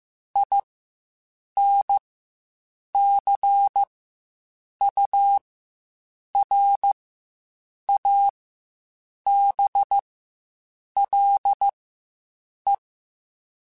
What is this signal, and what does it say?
Morse code, heard as a keyed tone, about 15 wpm, spaced as for 7 wpm, 800 Hz. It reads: INCURABLE
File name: Quotes Spaced / Quotes Spaced-mp3s / Optimist sp @10